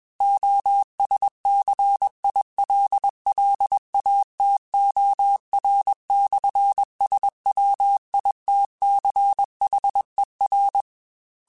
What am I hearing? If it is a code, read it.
Morse code, heard as a keyed tone, about 21 wpm, 800 Hz. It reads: OSCILLATOR/SWITCHER
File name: Technical Subjects / Technical Subjects-mp3s / Sw Pwr Sup @20